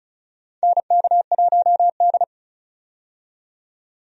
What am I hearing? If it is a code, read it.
Morse code, heard as a keyed tone, about 34 wpm, 700 Hz. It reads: NK1D
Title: USA Call Signs - 4 Characters - 35wpm